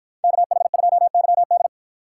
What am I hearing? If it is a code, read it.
Morse code, heard as a keyed tone, about 51 wpm, 700 Hz. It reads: KH2YD